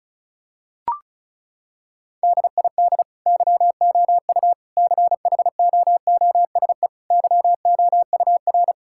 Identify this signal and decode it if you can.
Morse code, heard as a keyed tone, about 36 wpm, 700 Hz. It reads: DID YOU CHOOSE YOUR
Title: Sentences from Top 1000 Words - Review Part 6 - 35wpm